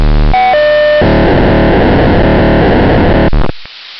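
POCSAG pager data.
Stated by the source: POCSAG pager data on 929.060 MHz.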